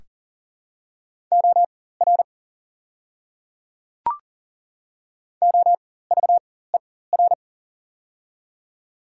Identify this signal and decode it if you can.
Morse code, heard as a keyed tone, about 40 wpm, spaced as for 19 wpm, 700 Hz. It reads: OR OVER
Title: 4x Character Spacing - Top 200 Words - 40wpm